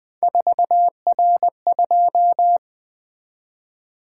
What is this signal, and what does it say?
Morse code, heard as a keyed tone, about 20 wpm, 700 Hz. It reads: 4R2
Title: Three Letter-Number Combinations - 20wpm